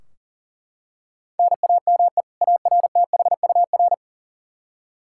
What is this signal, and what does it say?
Morse code, heard as a keyed tone, about 40 wpm, 700 Hz. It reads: NAME ARTHUR
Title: Practice QSO - Exchange 1 - Beginning - 40wpm